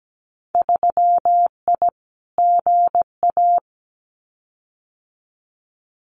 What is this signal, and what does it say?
Morse code, heard as a keyed tone, about 17 wpm, 700 Hz. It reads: 3I GA
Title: Winter Field Day Exchanges 17wpm